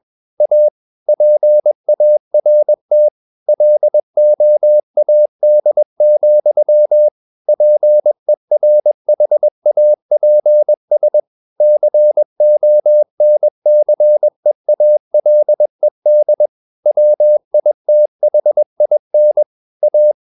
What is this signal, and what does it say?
Morse code, heard as a keyed tone, about 21 wpm, 600 Hz. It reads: A PART LOAD, PERHAPS CONCEALED WITHIN A
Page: CW med Gnister